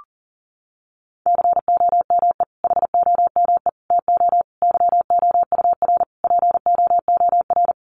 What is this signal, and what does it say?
Morse code, heard as a keyed tone, about 40 wpm, 700 Hz. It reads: COME HOME TO YOUR POOR
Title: Sentences from Top 1000 Words 40wpm